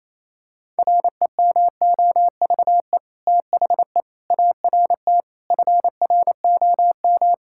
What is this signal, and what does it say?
Morse code, heard as a keyed tone, about 28 wpm, 700 Hz. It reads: REMOVE THE ART FROM
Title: Sentences from Top 1000 Words - Review Part 6 - 28wpm